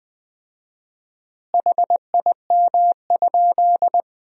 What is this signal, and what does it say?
Morse code, heard as a keyed tone, about 20 wpm, 700 Hz. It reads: HIM?
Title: 4x Word-Spacing - Sentences from Top 100 Words - 20wpm